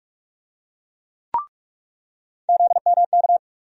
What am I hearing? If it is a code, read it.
Morse code, heard as a keyed tone, about 46 wpm, 700 Hz. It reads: ZMK
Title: USA Call Sign Postfixes - 3 Letters 45wpm